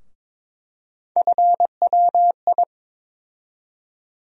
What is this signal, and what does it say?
Morse code, heard as a keyed tone, about 22 wpm, 700 Hz. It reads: FWI
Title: USA Call Sign Postfixes - 3 Letters 22wpm